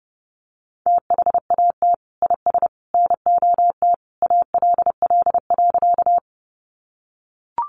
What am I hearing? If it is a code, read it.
Morse code, heard as a keyed tone, about 30 wpm, 700 Hz. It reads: THAT IS NOT ALL.
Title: Sentences from Top 600 Words 30wpm